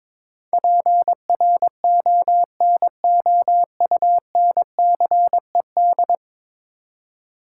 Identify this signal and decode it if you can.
Morse code, heard as a keyed tone, about 22 wpm, 700 Hz. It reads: PRONOUNCED
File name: practice / 22wpm / 10 Letter Words - First 3 Letters Given - 22wpm